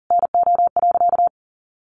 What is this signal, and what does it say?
Morse code, heard as a keyed tone, about 37 wpm, 700 Hz. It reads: NO.
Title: Sentences from Top 300 Words 40wpm